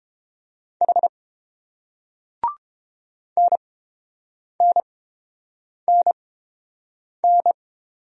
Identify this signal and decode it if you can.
Morse code, heard as a keyed tone, about 30 wpm, spaced as for 7 wpm, 700 Hz. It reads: H NNNN